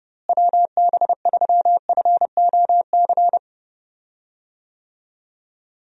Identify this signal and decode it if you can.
Morse code, heard as a keyed tone, about 31 wpm, 700 Hz. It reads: WB3FOC
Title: USA Call Signs - 6 Characters - Repeated 2x - 30wpm